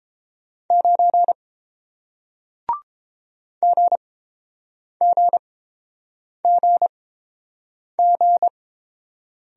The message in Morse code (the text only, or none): 9 GGGG